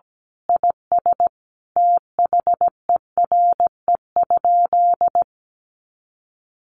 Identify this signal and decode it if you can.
Morse code, heard as a keyed tone, about 17 wpm, 700 Hz. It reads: IS THERE?
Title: Sentences from Top 1500 Words 17wpm